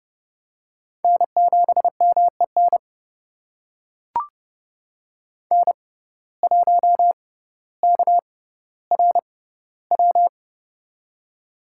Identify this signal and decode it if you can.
Morse code, heard as a keyed tone, about 30 wpm, spaced as for 15 wpm, 700 Hz. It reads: N7MEN N1KRW
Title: USA Call Signs - 5 Characters - 6x Character Spacing - 30wpm